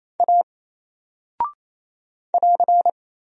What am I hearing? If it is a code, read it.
Morse code, heard as a keyed tone, about 28 wpm, 700 Hz. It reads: A <AR>
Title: Single Character (Letter:Number:Punctuation:Prosign) 28wpm